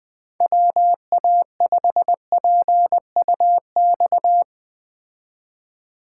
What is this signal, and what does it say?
Morse code, heard as a keyed tone, about 20 wpm, 700 Hz. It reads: WA5PUX